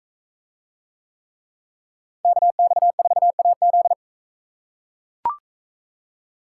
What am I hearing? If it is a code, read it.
Morse code, heard as a keyed tone, about 41 wpm, 700 Hz. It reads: KX4AZ